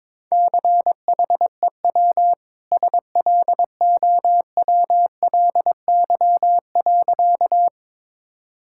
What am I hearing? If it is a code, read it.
Morse code, heard as a keyed tone, about 22 wpm, 700 Hz. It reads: CHEW SLOWLY.
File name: practice / 22wpm / Easy Sentences with Hints - 22wpm